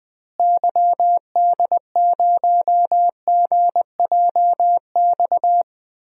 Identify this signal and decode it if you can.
Morse code, heard as a keyed tone, about 20 wpm, 700 Hz. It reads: YD0GJX
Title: DXCC-100 Call Signs - 6 Characters - Repeated 2x - 20wpm